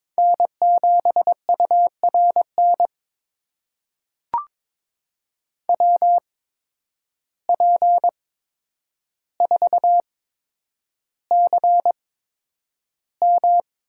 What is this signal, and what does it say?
Morse code, heard as a keyed tone, about 22 wpm, spaced as for 9 wpm, 700 Hz. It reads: N7URN WP4CM
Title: USA Call Signs - 5 Characters - 8x Character Spacing - 22wpm